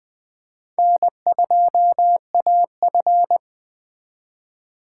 Morse code, heard as a keyed tone, about 21 wpm, 700 Hz. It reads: N2AF
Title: 2023 SOTA Chaser Call Signs - 20wpm